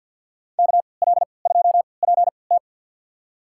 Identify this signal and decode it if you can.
Morse code, heard as a keyed tone, about 52 wpm, spaced as for 29 wpm, 690 Hz. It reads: KL2PT
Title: USA Call Signs - 5 Characters - 3x Character Spacing - 50wpm